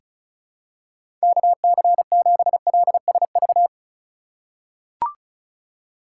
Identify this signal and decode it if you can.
Morse code, heard as a keyed tone, about 36 wpm, 700 Hz. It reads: KC7LSV
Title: USA Call Signs - 8x Character Spacing - 35wpm